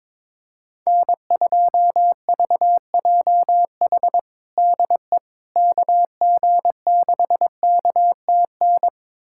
Morse code, heard as a keyed tone, about 21 wpm, 700 Hz. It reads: N2VJH DE KG6KTN
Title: USA Callsign DE Callsign 22wpm